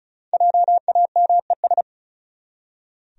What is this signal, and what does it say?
Morse code, heard as a keyed tone, about 34 wpm, 700 Hz. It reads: JAMES
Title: USA Common Names - 35wpm